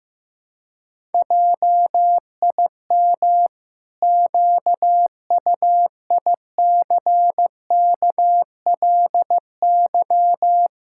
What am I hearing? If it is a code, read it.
Morse code, heard as a keyed tone, about 15 wpm, 700 Hz. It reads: JIM QUICKLY
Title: Pangram Challenge v2 - 15wpm